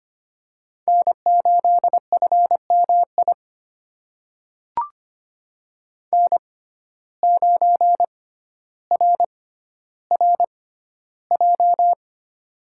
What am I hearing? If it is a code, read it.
Morse code, heard as a keyed tone, about 25 wpm, spaced as for 13 wpm, 700 Hz. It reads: N8FMI N9RRJ
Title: USA Call Signs - 5 Characters - 6x Character Spacing - 25wpm